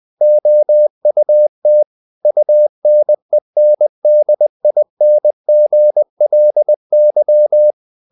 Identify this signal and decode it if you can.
Morse code, heard as a keyed tone, about 20 wpm, 600 Hz. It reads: OUT UNENDINGLY
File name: the old man and the sea / the old man and the sea part 1 20wpm